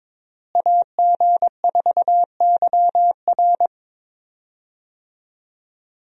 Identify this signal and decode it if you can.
Morse code, heard as a keyed tone, about 22 wpm, 700 Hz. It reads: AG4YR
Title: USA Call Signs - 5 Characters - Repeated 2x - 22wpm